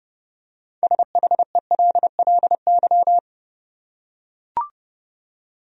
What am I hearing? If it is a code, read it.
Morse code, heard as a keyed tone, about 29 wpm, 700 Hz. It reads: SHELLY
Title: USA Common Names - 30wpm